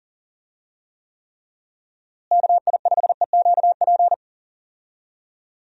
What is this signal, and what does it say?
Morse code, heard as a keyed tone, about 40 wpm, 700 Hz. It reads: KI5EQP
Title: USA Call Signs - 6 Characters - Repeated 3x - 40wpm